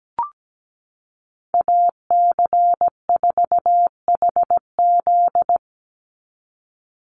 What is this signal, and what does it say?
Morse code, heard as a keyed tone, about 17 wpm, 700 Hz. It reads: AC4HZ
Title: USA Call Signs 17wpm